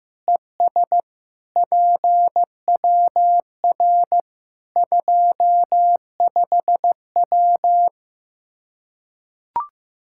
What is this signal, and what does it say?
Morse code, heard as a keyed tone, about 15 wpm, 700 Hz. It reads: ES PWR 25W
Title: Practice QSO - Exchange 2 - Info - 15wpm